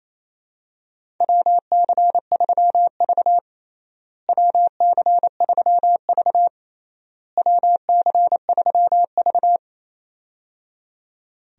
Morse code, heard as a keyed tone, about 28 wpm, 700 Hz. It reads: WC3V WC3V WC3V
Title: USA Call Signs - 4 Characters - Repeated 3x - 28wpm